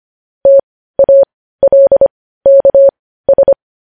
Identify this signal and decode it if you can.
Morse code, heard as a keyed tone, about 25 wpm, spaced as for 15 wpm, 550 Hz. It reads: TALKS
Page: news-headlines-in-morse-code-at-15-wpm